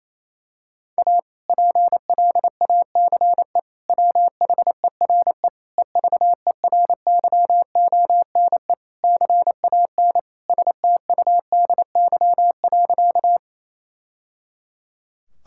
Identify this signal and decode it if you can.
Morse code, heard as a keyed tone, about 28 wpm, 700 Hz. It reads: A PLACE WHERE EVERYONE CAN STUDY.
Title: Sentences from Top 1000 Words - Review Part 5 - 28wpm